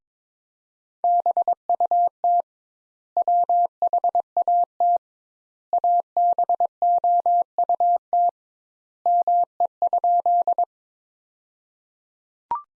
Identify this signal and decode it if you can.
Morse code, heard as a keyed tone, about 22 wpm, 700 Hz. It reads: BUT WHAT ABOUT ME?
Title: Sentences Top 200 Words - 22wpm